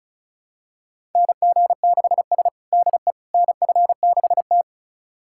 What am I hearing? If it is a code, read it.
Morse code, heard as a keyed tone, about 36 wpm, 700 Hz. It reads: NG6S DE NF6T
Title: USA Callsign DE Callsign - 4 Characters - 35wpm